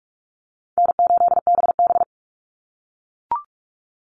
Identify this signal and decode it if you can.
Morse code, heard as a keyed tone, about 44 wpm, 700 Hz. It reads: N8BB
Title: NA SOTA Chaser Callsigns 45wpm